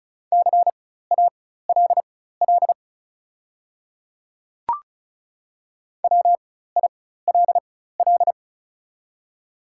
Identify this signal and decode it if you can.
Morse code, heard as a keyed tone, about 36 wpm, spaced as for 16 wpm, 700 Hz. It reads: CALL WILL
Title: 4x Character Spacing - Top 100 Words - 35wpm